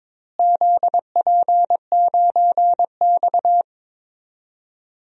Morse code, heard as a keyed tone, about 22 wpm, 700 Hz. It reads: ZP9X
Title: DXCC-100 Call Signs - 4 Characters - Repeated 3x - 22wpm